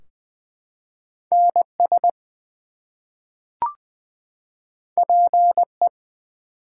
Morse code, heard as a keyed tone, about 20 wpm, 700 Hz. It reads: NS PE
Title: US States and Canadian Provinces 20wpm